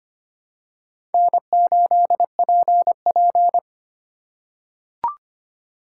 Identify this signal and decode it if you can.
Morse code, heard as a keyed tone, about 25 wpm, 700 Hz. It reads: N8PP
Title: USA Call Signs - 4 Characters - 3x Character Spacing - 25wpm